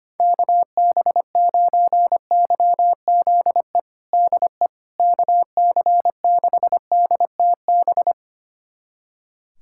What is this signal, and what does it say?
Morse code, heard as a keyed tone, about 25 wpm, 700 Hz. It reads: KB9YZE DE KC6DTB
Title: USA Callsign DE Callsign - 6 Characters - 25wpm